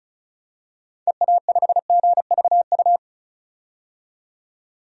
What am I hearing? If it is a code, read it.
Morse code, heard as a keyed tone, about 34 wpm, 700 Hz. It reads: EA5GVU
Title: DXCC-100 Call Signs - 6 Characters - 35wpm